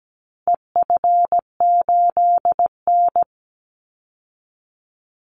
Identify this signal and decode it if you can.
Morse code, heard as a keyed tone, about 17 wpm, 700 Hz. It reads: EF8N
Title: Call Signs Repeated 3x - 17wpm